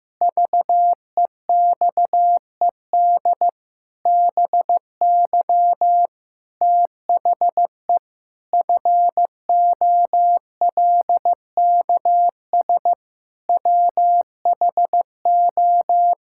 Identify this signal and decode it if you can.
Morse code, heard as a keyed tone, about 15 wpm, 700 Hz. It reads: VEXED BY THE FOLKS WHO
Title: Pangram Challenge v2 - 15wpm